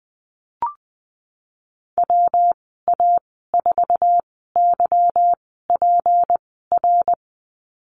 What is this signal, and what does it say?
Morse code, heard as a keyed tone, about 20 wpm, spaced as for 15 wpm, 700 Hz. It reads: WA4YPR
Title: 186 Lesson 30g - Letter b - Call Signs 20wpm